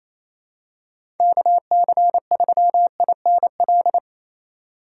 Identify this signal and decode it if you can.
Morse code, heard as a keyed tone, about 28 wpm, 700 Hz. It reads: KC3INL
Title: USA Call Signs - 2x Character Spacing - 28wpm